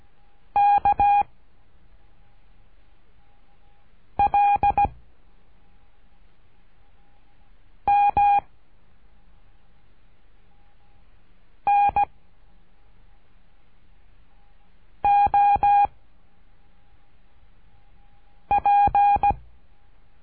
Morse code, heard as a keyed tone, about 17 wpm, 810 Hz. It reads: K L M N O P
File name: cursos / CURSO DE TELEGRAFIA / Telegrafia - Trilha 13